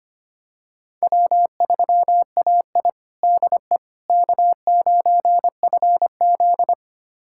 Morse code, heard as a keyed tone, about 25 wpm, 700 Hz. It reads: W3AI DE K9FZ
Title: USA Callsign DE Callsign - 4 Characters - 25wpm